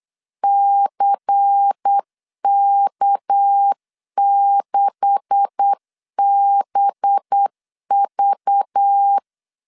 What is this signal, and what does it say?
Morse code, heard as a keyed tone, about 8 wpm, 800 Hz. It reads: CK6BV
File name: LEZIONI / 9 286 8wpm 3425